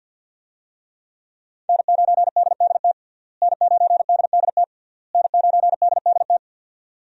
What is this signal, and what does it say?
Morse code, heard as a keyed tone, about 51 wpm, 700 Hz. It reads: N9DDT N9DDT N9DDT
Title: USA Call Signs - 5 Characters - Repeated 3x - 50wpm